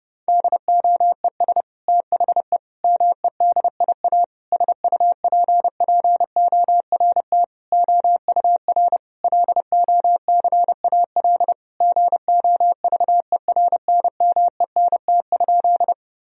Morse code, heard as a keyed tone, about 32 wpm, 700 Hz. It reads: DOES THE MEDIA SUPPORT OUR LOCAL GOVERNMENT?
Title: Sentences from Top 1000 Words - Review Part 4 - 30wpm